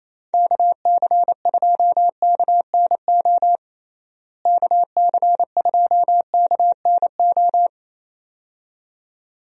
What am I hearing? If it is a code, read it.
Morse code, heard as a keyed tone, about 27 wpm, 700 Hz. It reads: KC2KNO KC2KNO
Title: USA Call Signs - 6 Characters - Repeated 2x - 28wpm